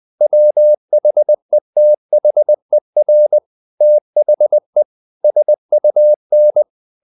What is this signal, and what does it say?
Morse code, heard as a keyed tone, about 20 wpm, 600 Hz. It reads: WHETHER THE SUN
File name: the old man and the sea / the old man and the sea part 1 20wpm